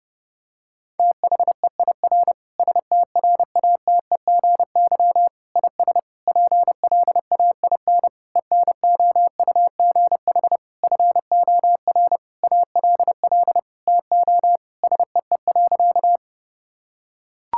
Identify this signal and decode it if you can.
Morse code, heard as a keyed tone, about 30 wpm, 700 Hz. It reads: THEIR STRATEGY IS PLAIN ENOUGH FOR ALL TO SEE.
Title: Sentences from Top 1000 Words - Review Part 5 - 30wpm